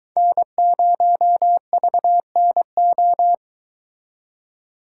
Morse code, heard as a keyed tone, about 23 wpm, 700 Hz. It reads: N0VNO